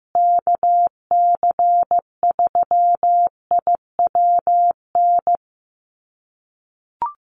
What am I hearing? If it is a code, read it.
Morse code, heard as a keyed tone, about 15 wpm, 700 Hz. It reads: KC3IWN
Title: USA Call Signs Repeated 2x 15wpm